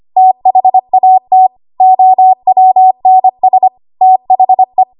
Morse code, heard as a keyed tone, about 25 wpm, 750 Hz. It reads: THAT OWNS THE
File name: Archive / 25 WPM / 25WPM